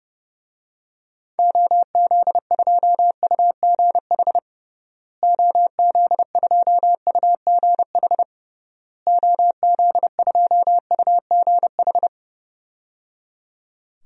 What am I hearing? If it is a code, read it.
Morse code, heard as a keyed tone, about 30 wpm, 700 Hz. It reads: OZ2UGH OZ2UGH OZ2UGH
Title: DXCC-100 Call Signs - 6 Characters - Repeated 3x - 30wpm